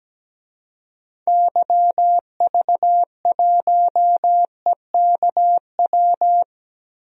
Morse code, heard as a keyed tone, about 17 wpm, 700 Hz. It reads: YV1EKW
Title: DXCC-100 Call Signs - 6 Characters - Repeated 3x - 17wpm